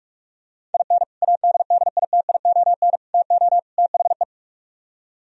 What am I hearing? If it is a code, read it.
Morse code, heard as a keyed tone, about 44 wpm, 700 Hz. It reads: IN ADDITION TO THE